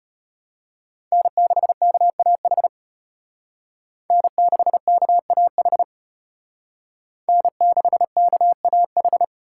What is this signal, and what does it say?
Morse code, heard as a keyed tone, about 33 wpm, 700 Hz. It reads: N6KAH N6KAH N6KAH